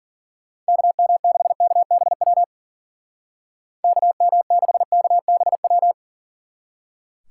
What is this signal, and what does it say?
Morse code, heard as a keyed tone, about 51 wpm, 700 Hz. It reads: KM6KBW KM6KBW